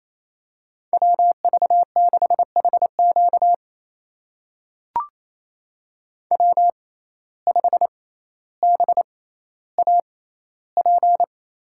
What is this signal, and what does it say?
Morse code, heard as a keyed tone, about 28 wpm, spaced as for 14 wpm, 700 Hz. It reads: WV6HQ W5BAP
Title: USA Call Signs - 5 Characters - 6x Character Spacing - 28wpm